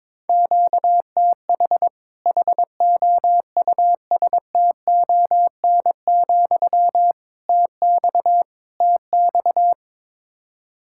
Morse code, heard as a keyed tone, about 22 wpm, 700 Hz. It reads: QTH HOUSTON, TX TX
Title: Practice QSO - Exchange 1 - Beginning - 22wpm